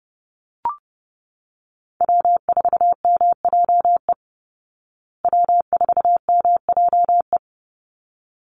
Morse code, heard as a keyed tone, about 30 wpm, 700 Hz. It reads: W4MJE W4MJE
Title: Call Signs Repeated 3x - 30wpm